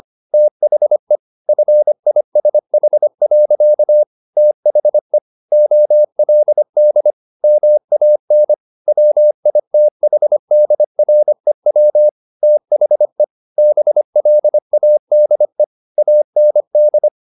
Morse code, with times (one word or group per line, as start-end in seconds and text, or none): 0.3-1.2 THE
1.5-4.0 FISH.
4.4-5.2 THE
5.5-7.1 OLD
7.4-8.5 MAN
8.9-12.1 WITHDREW
12.4-13.2 THE
13.6-15.6 BLADE
16.0-17.1 AND